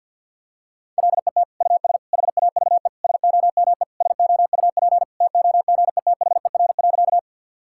Morse code, heard as a keyed tone, about 47 wpm, 700 Hz. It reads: LET US HAVE SOME SOUP TOGETHER.
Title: Sentences from Top 1000 Words - Review Part 6 - 50wpm